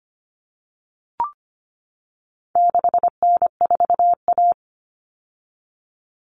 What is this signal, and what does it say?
Morse code, heard as a keyed tone, about 25 wpm, 700 Hz. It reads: 6N4A
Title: Common Cut Numbers - 4-Digit 25wpm